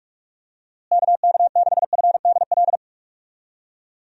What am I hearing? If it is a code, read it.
Morse code, heard as a keyed tone, about 46 wpm, 700 Hz. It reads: KK6FDL
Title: USA Call Signs - 6 Characters - Repeated 3x - 45wpm